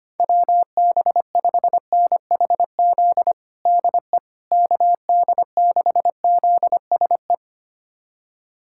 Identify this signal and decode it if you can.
Morse code, heard as a keyed tone, about 25 wpm, 700 Hz. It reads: WB5NHZ DE KD6ZSE
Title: USA Callsign DE Callsign 25wpm